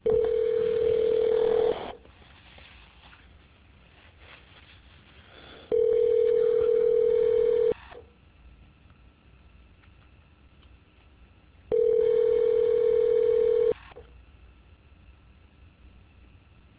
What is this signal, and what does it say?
Ringback tone (North America), 440 + 480 Hz.